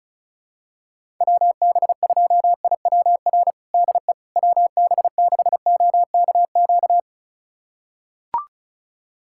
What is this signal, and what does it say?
Morse code, heard as a keyed tone, about 34 wpm, 700 Hz. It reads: WB2IWR DE WB6OKQ
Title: USA Callsign DE Callsign 35wpm